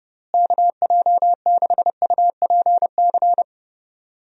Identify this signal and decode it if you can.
Morse code, heard as a keyed tone, about 30 wpm, 700 Hz. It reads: KJ6UPC
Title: USA Call Signs - 6 Characters - Repeated 2x - 30wpm